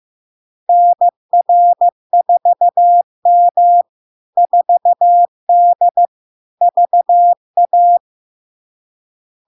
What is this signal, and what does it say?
Morse code, heard as a keyed tone, about 15 wpm, 700 Hz. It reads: NR4M 4D VA
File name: LARK-FD-2022-Log-in-CW-15wpm-700-Hz